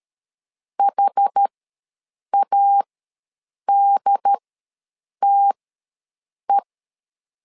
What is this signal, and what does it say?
Morse code, heard as a keyed tone, about 13 wpm, spaced as for 7 wpm, 800 Hz. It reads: HADTE
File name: LEZIONI / 4 Q ws 12wpm 1912